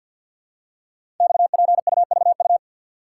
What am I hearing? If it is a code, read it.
Morse code, heard as a keyed tone, about 50 wpm, 690 Hz. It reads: XPUVU
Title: Code Groups - 5 Letters - 50wpm